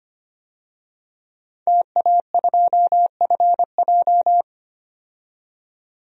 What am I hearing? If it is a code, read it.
Morse code, heard as a keyed tone, about 25 wpm, 690 Hz. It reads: TA2FJ
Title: DXCC-100 Call Signs - 5 Characters - 25wpm